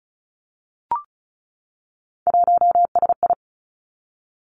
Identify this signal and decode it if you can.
Morse code, heard as a keyed tone, about 36 wpm, 700 Hz. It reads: 1SI
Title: Three Challenging Letter-Number Combinations 35wpm